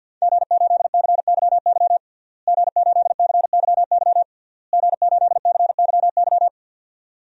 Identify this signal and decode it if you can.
Morse code, heard as a keyed tone, about 49 wpm, 700 Hz. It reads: G8CYY G8CYY G8CYY
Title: DXCC-100 Call Signs - 5 Characters - Repeated 3x - 50wpm